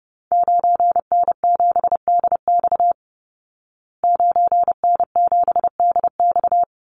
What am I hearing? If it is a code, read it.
Morse code, heard as a keyed tone, about 30 wpm, 700 Hz. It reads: 9N7DX 9N7DX
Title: Call Signs Repeated 3x - 30wpm